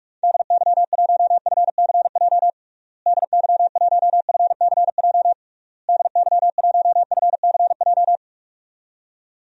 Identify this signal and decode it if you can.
Morse code, heard as a keyed tone, about 46 wpm, 700 Hz. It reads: DY1FCJ DY1FCJ DY1FCJ
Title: DXCC-100 Call Signs - 6 Characters - Repeated 3x - 45wpm